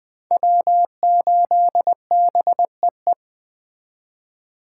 Morse code, heard as a keyed tone, about 21 wpm, 700 Hz. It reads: W8BEE